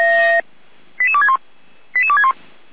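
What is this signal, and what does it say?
Five-tone selective call (ZVEI), heard as quick runs of five tones, tones of about 70 ms. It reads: 88261 88261
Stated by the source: ZVEI five-tone selective call 88335 received by SDR, from the audio archive of feuerwehr-orlamuende.org fire brigade.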